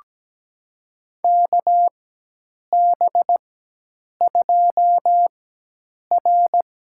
Morse code, heard as a keyed tone, about 17 wpm, spaced as for 8 wpm, 700 Hz. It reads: KB2R
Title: USA Call Signs - 4x Character Spacing - 17wpm